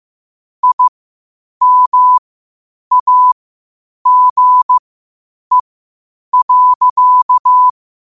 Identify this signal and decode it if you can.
Morse code, heard as a keyed tone, about 15 wpm, spaced as for 8 wpm, 1000 Hz. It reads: IMAGE.
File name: qotdfiles / 8wpm